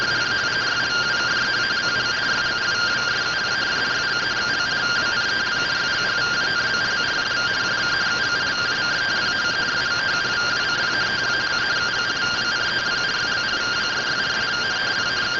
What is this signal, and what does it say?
Two-tone teleprinter signal, about 170 Hz shift.